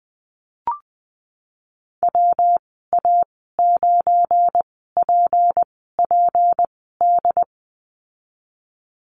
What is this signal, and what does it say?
Morse code, heard as a keyed tone, about 20 wpm, spaced as for 15 wpm, 700 Hz. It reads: WA9PPD
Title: 207 Lesson 33g - Letter j - Call Signs 20wpm